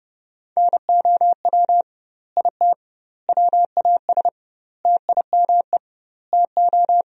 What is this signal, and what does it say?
Morse code, heard as a keyed tone, about 30 wpm, 700 Hz. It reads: NOW IT WAS TIME TO
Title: 2x Word-Spacing - Sentences from Top 200 Words - 30wpm